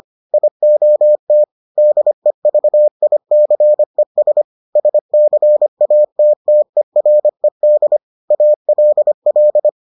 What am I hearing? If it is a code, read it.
Morse code, heard as a keyed tone, about 25 wpm, 600 Hz. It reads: IOT DEVICES SCATTERED ALL